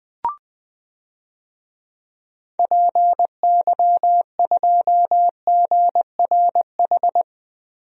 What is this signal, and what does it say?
Morse code, heard as a keyed tone, about 21 wpm, 700 Hz. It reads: PY2GRH
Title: DXCC-100 Call Signs - 6 Characters - Repeated 2x - 20wpm